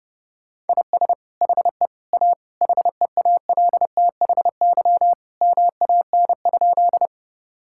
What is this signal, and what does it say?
Morse code, heard as a keyed tone, about 32 wpm, 700 Hz. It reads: IS HE A HEALTHY MAN?
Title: Sentences from Top 1000 Words - Review Part 5 - 30wpm